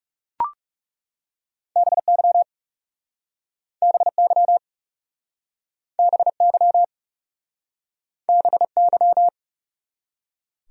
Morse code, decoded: BY BY BY BY

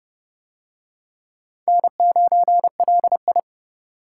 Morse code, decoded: N9LI

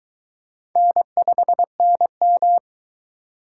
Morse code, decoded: N5NM

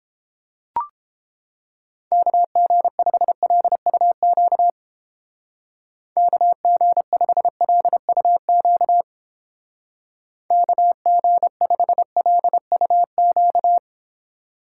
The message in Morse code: KG5LUQ KG5LUQ KG5LUQ

